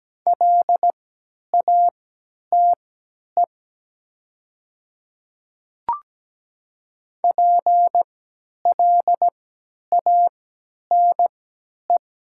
Morse code, heard as a keyed tone, about 17 wpm, spaced as for 10 wpm, 700 Hz. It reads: LATE PLANE
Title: 3x Character Spacing - Top 500 Words - 17wpm